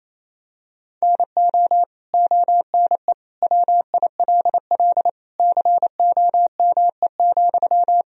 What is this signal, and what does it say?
Morse code, heard as a keyed tone, about 28 wpm, 700 Hz. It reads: NO ONE WILL COME,
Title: Sentences from Top 100 Words and Abbreviations 28wpm